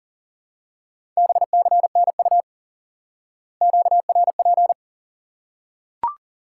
Morse code, heard as a keyed tone, about 40 wpm, 700 Hz. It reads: BCNU QRP
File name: IQE Course 3 - Lesson 89 - YAGI - Two Elements - 40wpm